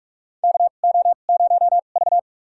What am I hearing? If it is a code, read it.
Morse code, heard as a keyed tone, about 46 wpm, spaced as for 33 wpm, 700 Hz. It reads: KO0V
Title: USA Call Signs - 4 Characters - 2x Character Spacing - 45wpm